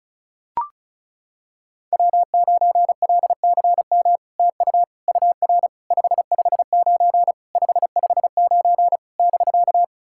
Morse code, heard as a keyed tone, about 36 wpm, 700 Hz. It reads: W9LCM TU UR 559 559 <BK>
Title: SOTA QSO Practice 35wpm